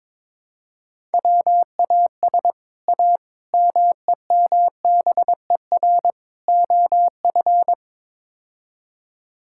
Morse code, heard as a keyed tone, about 22 wpm, 700 Hz. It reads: WAS A MEMBER OF
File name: practice / 22wpm / Build A Phrase - 5-Word Phrases - 22wpm